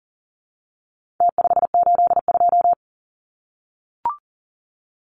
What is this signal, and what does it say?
Morse code, heard as a keyed tone, about 40 wpm, 700 Hz. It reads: T582
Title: Numbers - Contest Style - 4-Digit 40wpm